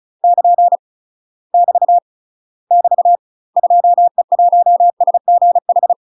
Morse code, heard as a keyed tone, about 34 wpm, 700 Hz. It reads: <KN> <BT> <BT> 2E1SGH
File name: UK MASTER QSOs TRAINING/35WPM